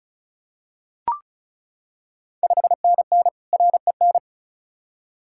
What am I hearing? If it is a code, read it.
Morse code, heard as a keyed tone, about 36 wpm, 690 Hz. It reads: 5NN REN
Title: Ontario QSO Party Exchange 35wpm